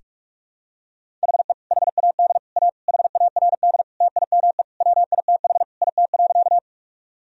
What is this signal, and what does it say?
Morse code, heard as a keyed tone, about 46 wpm, 700 Hz. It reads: HE HAD A HARD TIME WITH IT.
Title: Sentences from Top 1000 Words - Review Part 1 - 45wpm